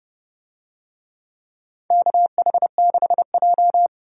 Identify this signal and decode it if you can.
Morse code, heard as a keyed tone, about 30 wpm, 700 Hz. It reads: KH6J
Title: DXCC-100 Call Signs - 4 Characters - 30wpm